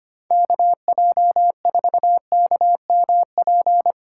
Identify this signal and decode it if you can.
Morse code, heard as a keyed tone, about 31 wpm, 700 Hz. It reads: KJ4KMP
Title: Speed Racing - USA 2x3 Call Signs 25wpm